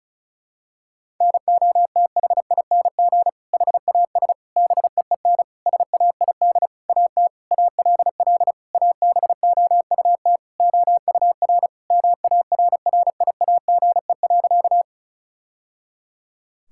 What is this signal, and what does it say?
Morse code, heard as a keyed tone, about 36 wpm, 700 Hz. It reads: NOTHING HAS BEEN SAID AT ALL ABOUT OUR MARRIAGE.
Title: Sentences from Top 1000 Words - Review Part 5 - 35wpm